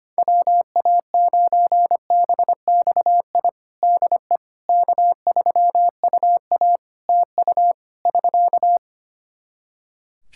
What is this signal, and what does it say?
Morse code, heard as a keyed tone, about 25 wpm, 700 Hz. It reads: WA9BXI DE K3UA TU <SK>
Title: K1USN SST QSO Practice 25wpm